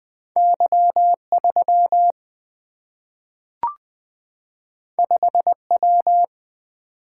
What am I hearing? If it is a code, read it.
Morse code, heard as a keyed tone, about 20 wpm, 700 Hz. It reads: Y3 5W
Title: Single-Letter Single-Number 20wpm